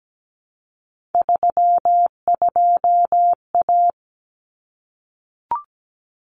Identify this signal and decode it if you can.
Morse code, heard as a keyed tone, about 17 wpm, 700 Hz. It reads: 32A
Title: Common Cut Numbers - 3-Digit 17wpm